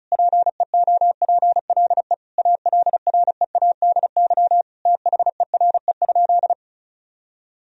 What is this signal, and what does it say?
Morse code, heard as a keyed tone, about 36 wpm, 700 Hz. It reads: PEOPLE ALREADY THERE?
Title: Sentences from Top 1000 Words - Review Part 4 - 35wpm